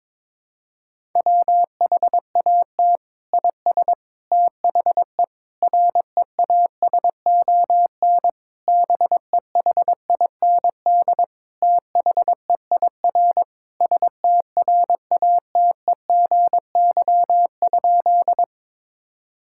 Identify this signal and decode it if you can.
Morse code, heard as a keyed tone, about 22 wpm, 700 Hz. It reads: WHAT IS THE REASON BEHIND THEIR STRATEGY?
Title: Sentences from Top 1000 Words - Review Part 5 - 22wpm